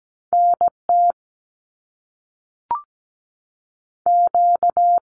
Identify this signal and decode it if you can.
Morse code, heard as a keyed tone, about 17 wpm, spaced as for 5 wpm, 700 Hz. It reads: NTQ
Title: Canadian Provinces and Territories 17wpm